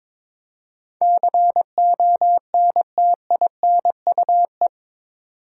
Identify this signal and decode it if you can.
Morse code, heard as a keyed tone, about 22 wpm, 700 Hz. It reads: CONTINUE
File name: practice / 22wpm / Eight Letter Words - 22wpm